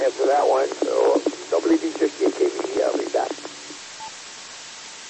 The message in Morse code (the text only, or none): none